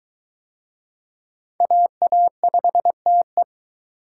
Morse code, heard as a keyed tone, about 23 wpm, 700 Hz. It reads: AA5TE